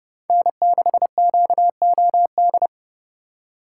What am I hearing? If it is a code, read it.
Morse code, heard as a keyed tone, about 30 wpm, 700 Hz. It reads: N6QOD
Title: USA Call Signs - 5 Characters - 3x Character Spacing - 30wpm